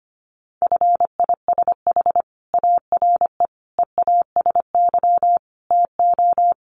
Morse code, heard as a keyed tone, about 25 wpm, 700 Hz. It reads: FISH ARE EASY TO
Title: Additional Easy Phrases 25wpm